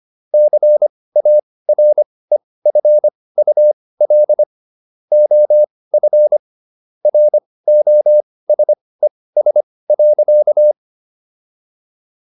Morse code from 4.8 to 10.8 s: OF ROSES.

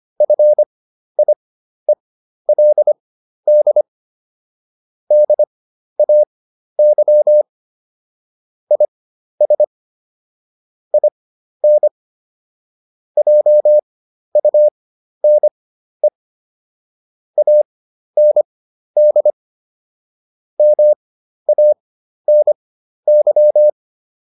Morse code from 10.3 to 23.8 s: IN JUNE AND MANY

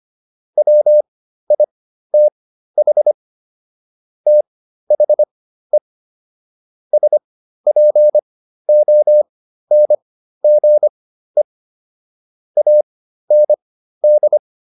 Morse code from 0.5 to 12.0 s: WITH THE SPONGE

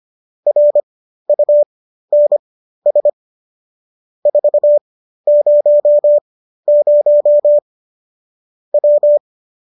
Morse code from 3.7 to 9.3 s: 400 W

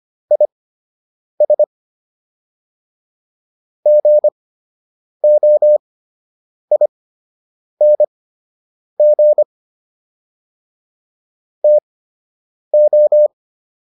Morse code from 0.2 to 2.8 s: IS

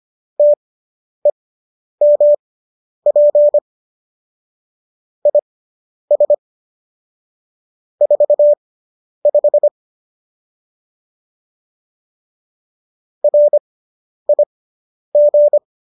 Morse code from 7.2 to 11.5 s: 45

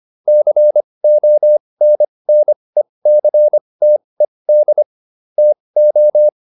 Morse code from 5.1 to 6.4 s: TO

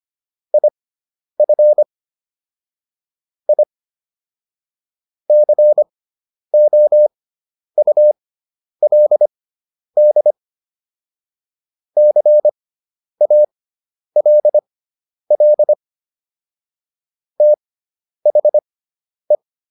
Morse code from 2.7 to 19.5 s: I COULD CALL THE

